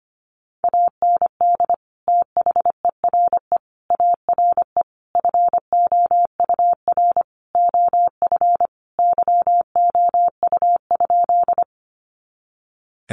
Morse code, heard as a keyed tone, about 25 wpm, 700 Hz. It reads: AND THERE ARE FOUR OF YOU?
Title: Sentences from Top 300 Words 25wpm